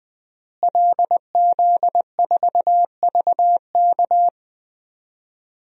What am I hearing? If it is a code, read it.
Morse code, heard as a keyed tone, about 20 wpm, 700 Hz. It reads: LZ4VK
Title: DXCC-100 Call Signs - 5 Characters - Repeated 3x - 20wpm